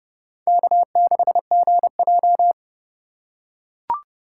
Morse code, heard as a keyed tone, about 30 wpm, 700 Hz. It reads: K6GJ
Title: USA Call Signs - 4 Characters - 2x Character Spacing - 30wpm